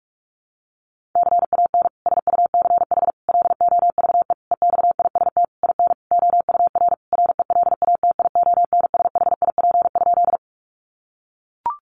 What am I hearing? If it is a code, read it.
Morse code, heard as a keyed tone, about 44 wpm, 700 Hz. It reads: CAN SUCH LOVE EXIST IN OUR RELATIONSHIP?
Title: Sentences from Top 1100 Words 45wpm